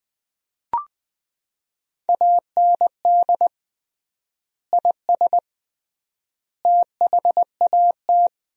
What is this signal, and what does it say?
Morse code, heard as a keyed tone, about 20 wpm, 700 Hz. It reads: AND IS THAT
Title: 3x Word-Spacing - Sentences from Top 100 Words - 20wpm